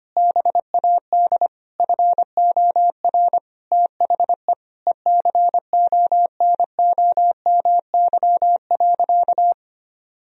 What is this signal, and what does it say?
Morse code, heard as a keyed tone, about 25 wpm, 700 Hz. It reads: BAD FOR THE ECONOMY.
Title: Sentences from Top 1000 Words - Review Part 3 - 25wpm